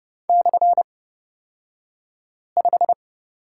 Morse code, heard as a keyed tone, about 30 wpm, spaced as for 5 wpm, 700 Hz. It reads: /5
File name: Lesson 36d - BT - Two Characters 30wpm